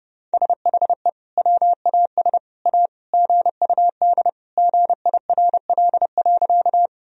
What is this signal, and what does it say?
Morse code, heard as a keyed tone, about 30 wpm, 700 Hz. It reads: SHE WAS A GUD GIRL.
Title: Sentences from Top 400 Words and Abbreviations 30wpm